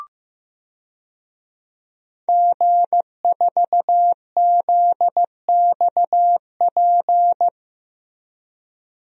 Morse code, heard as a keyed tone, about 15 wpm, 700 Hz. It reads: G4ZXP
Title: DXCC-100 Call Signs - 5 Characters - 15wpm